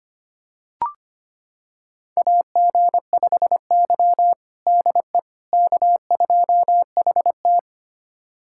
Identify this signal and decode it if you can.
Morse code, heard as a keyed tone, about 25 wpm, 700 Hz. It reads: AG5Y DE K2HT
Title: USA Callsign DE Callsign - 4 Characters - 25wpm